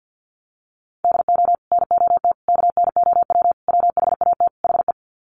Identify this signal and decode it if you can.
Morse code, heard as a keyed tone, about 50 wpm, 700 Hz. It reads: DO NOT KNOW WHAT HE
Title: Sets of 5 Words Encore2 50wpm